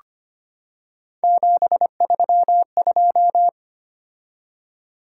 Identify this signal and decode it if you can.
Morse code, heard as a keyed tone, about 25 wpm, 700 Hz. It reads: 732
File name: Three Letter-Number Combinations - 25wpm